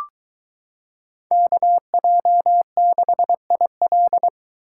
Morse code, heard as a keyed tone, about 23 wpm, 700 Hz. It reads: KJ6IL